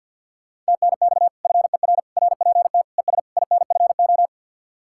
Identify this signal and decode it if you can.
Morse code, heard as a keyed tone, about 62 wpm, 700 Hz. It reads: TNX FER RPT ES INFO
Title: Speed-Racing Common QSO Phrases - 50wpm